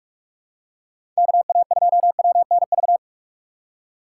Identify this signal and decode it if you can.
Morse code, heard as a keyed tone, about 47 wpm, 700 Hz. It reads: KA2WNV
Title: USA Call Signs - 6 Characters - 4x Character Spacing - 45wpm